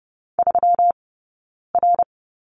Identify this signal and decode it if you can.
Morse code, heard as a keyed tone, about 30 wpm, 700 Hz. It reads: 3 R
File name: Lesson 20d - Number 3 - Two Characters 30wpm